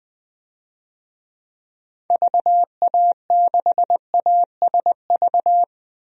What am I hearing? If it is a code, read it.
Morse code, heard as a keyed tone, about 20 wpm, 700 Hz. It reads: VA6ASV
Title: DXCC-100 Call Signs - 6 Characters - Repeated 2x - 20wpm